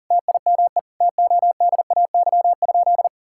Morse code, heard as a keyed tone, about 40 wpm, 700 Hz. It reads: TIME TODAY?
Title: Sentences from Top 1000 Words - Review Part 5 - 40wpm